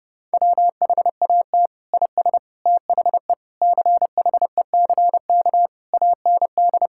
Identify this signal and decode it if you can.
Morse code, heard as a keyed tone, about 30 wpm, 700 Hz. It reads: WHAT IS THE CHECK AND
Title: Sentences from Top 1000 Words - Review Part 6 - 30wpm